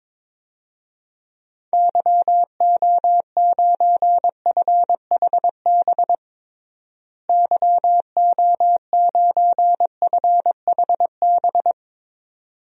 Morse code, heard as a keyed tone, about 22 wpm, 700 Hz. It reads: YO9FHB YO9FHB
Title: DXCC-100 Call Signs - 6 Characters - Repeated 2x - 22wpm